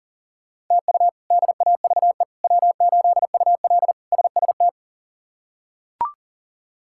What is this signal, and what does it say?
Morse code, heard as a keyed tone, about 37 wpm, 700 Hz. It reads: TU DAVE W8UL SST
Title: K1USN SST QSO Practice - 40wpm